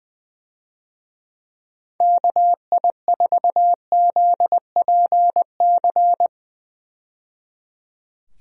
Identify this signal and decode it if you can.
Morse code, heard as a keyed tone, about 20 wpm, 700 Hz. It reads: KI4ZPC
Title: USA Call Signs - 6 Characters - 20wpm